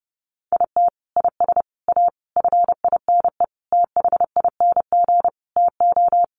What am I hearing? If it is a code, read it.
Morse code, heard as a keyed tone, about 30 wpm, 700 Hz. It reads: IT IS A FINE THING TO
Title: Sentences from Top 1000 Words 30wpm